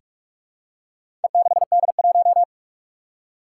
Morse code, heard as a keyed tone, about 46 wpm, 700 Hz. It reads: E6D1
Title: Four Challenging Letter-Number Combinations 45wpm